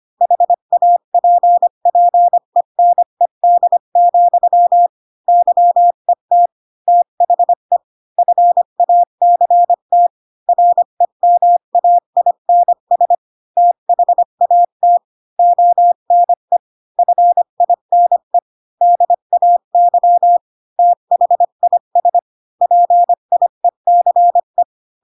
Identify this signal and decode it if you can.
Morse code, heard as a keyed tone, about 25 wpm, 700 Hz. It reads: HAPPENED, YET THE FACT REMAINS THAT ONE FINE DAY THIS PIECE
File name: Pinocchio / Pinocchio ch1 20 25 ik1hge